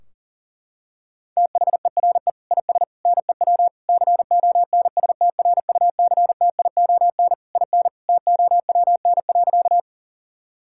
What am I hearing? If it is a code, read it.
Morse code, heard as a keyed tone, about 40 wpm, 700 Hz. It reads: THERE IS NEW CONSTRUCTION IN TOWN.
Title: Sentences from Top 1600 Words 40wpm